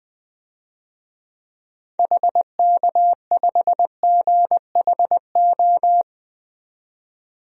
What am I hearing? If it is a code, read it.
Morse code, heard as a keyed tone, about 20 wpm, 700 Hz. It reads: HK5GHO
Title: DXCC-100 Call Signs - 6 Characters - 20wpm